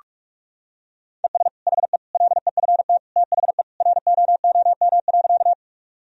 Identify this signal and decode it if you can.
Morse code, heard as a keyed tone, about 46 wpm, 700 Hz. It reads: ES HE LEFT THE ROOM.
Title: Sentences from Top 300 Words and Abbreviations 45wpm